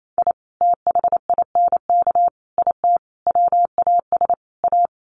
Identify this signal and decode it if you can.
Morse code, heard as a keyed tone, about 28 wpm, 700 Hz. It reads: I THINK IT WAS A
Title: Sets of 5 Words 28wpm